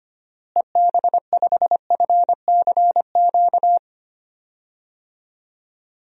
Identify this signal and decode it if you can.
Morse code, heard as a keyed tone, about 25 wpm, 700 Hz. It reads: EB5FCQ
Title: DXCC-100 Call Signs - 6 Characters - Repeated 2x - 25wpm